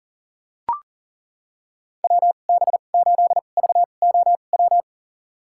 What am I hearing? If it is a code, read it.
Morse code, heard as a keyed tone, about 40 wpm, spaced as for 29 wpm, 700 Hz. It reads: WB8VOW
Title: USA Call Signs - 6 Characters - 2x Character Spacing - 40wpm